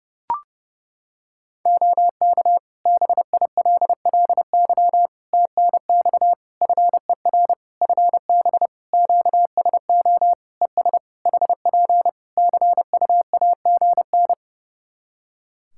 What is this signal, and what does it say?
Morse code, heard as a keyed tone, about 32 wpm, 700 Hz. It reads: OK BILLY TNX FER FB QSO ES HP CUAGN
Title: Practice QSO - Exchange 4 - Ending - 30wpm